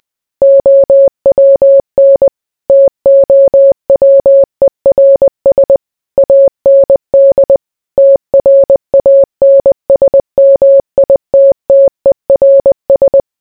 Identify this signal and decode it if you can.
Morse code, heard as a keyed tone, about 20 wpm, 550 Hz. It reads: OWN TOWERS AND TRANSMITTERS